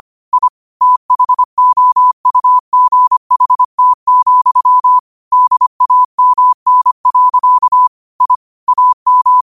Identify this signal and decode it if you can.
Morse code, heard as a keyed tone, about 25 wpm, 1000 Hz. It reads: I THOUGHT, DAMN. I AM
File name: qotdfiles / 25wpm